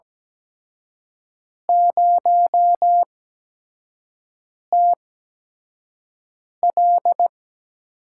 Morse code, heard as a keyed tone, about 17 wpm, spaced as for 5 wpm, 700 Hz. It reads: 0TL
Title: USA Call Signs - 8x Character Spacing - 17wpm